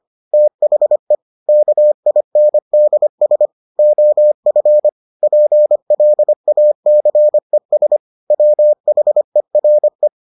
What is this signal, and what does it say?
Morse code, heard as a keyed tone, about 25 wpm, 600 Hz. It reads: THE KINDS OF PLACES WHERE